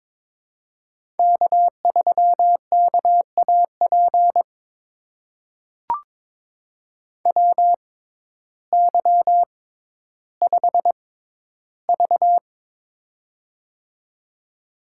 Morse code, heard as a keyed tone, about 22 wpm, spaced as for 12 wpm, 700 Hz. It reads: K3KAP WY5V
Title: USA Call Signs - 6x Character Spacing - 22wpm